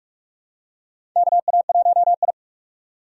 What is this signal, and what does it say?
Morse code, heard as a keyed tone, about 43 wpm, 700 Hz. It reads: KA1I
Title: USA Call Signs - 2x Character Spacing - 45wpm